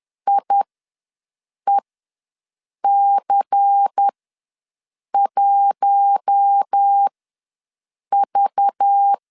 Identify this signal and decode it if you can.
Morse code, heard as a keyed tone, about 10 wpm, spaced as for 6 wpm, 800 Hz. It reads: IEC1V
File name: LEZIONI / 5 LBVC / 5 LBVC ws 10wpm 3733